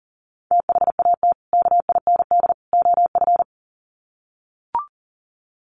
Morse code, heard as a keyed tone, about 40 wpm, 700 Hz. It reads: THAT KIND OF
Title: Sets of 3 Words 40wpm